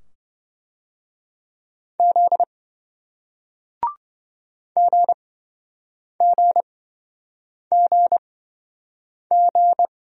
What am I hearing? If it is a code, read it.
Morse code, heard as a keyed tone, about 28 wpm, spaced as for 7 wpm, 700 Hz. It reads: Z GGGG